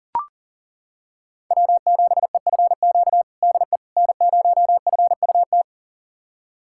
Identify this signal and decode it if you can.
Morse code, heard as a keyed tone, about 40 wpm, 700 Hz. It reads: W7EFQ DE N0FUT
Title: USA Callsign DE Callsign - 5 Characters - 40wpm